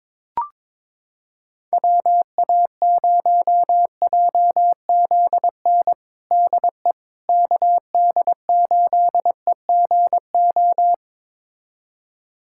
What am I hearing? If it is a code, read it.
Morse code, heard as a keyed tone, about 22 wpm, 700 Hz. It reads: WA0JZN DE KD8EGO
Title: USA Callsign DE Callsign 22wpm